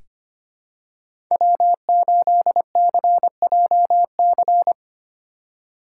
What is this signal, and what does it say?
Morse code, heard as a keyed tone, about 25 wpm, 700 Hz. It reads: W8CJC
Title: USA Call Signs - 5 Characters - 3x Character Spacing - 25wpm